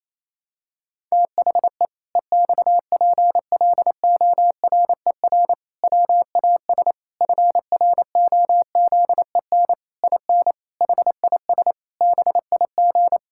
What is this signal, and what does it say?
Morse code, heard as a keyed tone, about 27 wpm, 700 Hz. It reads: THE EXPLORER WAS FROZEN IN HIS BIG
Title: Pangram Challenge v2 - 28wpm